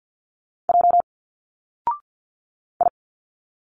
Morse code, heard as a keyed tone, about 50 wpm, 700 Hz. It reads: J I
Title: Single Letters 50wpm